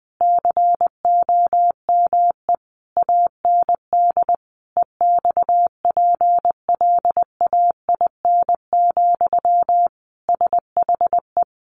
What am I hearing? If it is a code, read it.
Morse code, heard as a keyed tone, about 19 wpm, 700 Hz. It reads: COME AND EXPLAIN, SHE
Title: Sentences from Top 1100 Words 20wpm